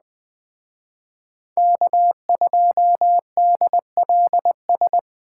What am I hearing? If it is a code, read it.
Morse code, heard as a keyed tone, about 20 wpm, 700 Hz. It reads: K2DLS
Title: Speed Racing - USA 1x3 Call Signs 17wpm